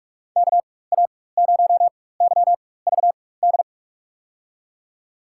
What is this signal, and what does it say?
Morse code, heard as a keyed tone, about 45 wpm, spaced as for 21 wpm, 700 Hz. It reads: KA0YVD
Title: USA Call Signs - 6 Characters - 4x Character Spacing - 45wpm